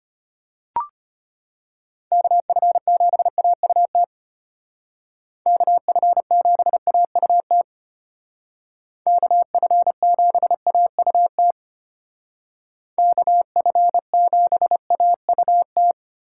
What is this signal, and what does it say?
Morse code, heard as a keyed tone, about 33 wpm, 700 Hz. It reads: KF7AUT KF7AUT KF7AUT KF7AUT